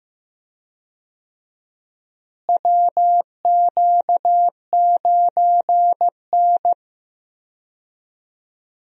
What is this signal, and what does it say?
Morse code, heard as a keyed tone, about 15 wpm, 700 Hz. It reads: WQ9N
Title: USA Call Signs - 4 Characters - Repeated 2x - 15wpm